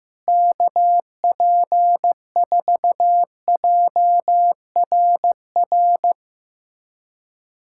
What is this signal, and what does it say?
Morse code, heard as a keyed tone, about 15 wpm, 700 Hz. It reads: KP4JRR
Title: USA Callsign DE Callsign - 6 Characters - 15wpm